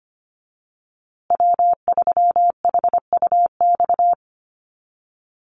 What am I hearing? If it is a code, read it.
Morse code, heard as a keyed tone, about 25 wpm, 700 Hz. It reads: W3HUX
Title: USA Call Signs Repeated 2x 25wpm